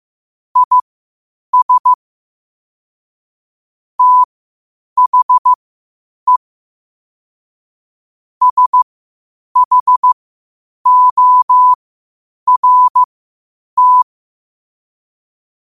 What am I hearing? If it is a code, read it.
Morse code, heard as a keyed tone, about 15 wpm, spaced as for 8 wpm, 1000 Hz. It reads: IS THE SHORT
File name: qotdfiles / 8wpm